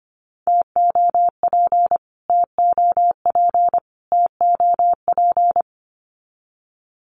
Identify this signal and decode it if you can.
Morse code, heard as a keyed tone, about 25 wpm, 700 Hz. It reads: TOP TOP TOP
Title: Top 500 Words Repeated 25wpm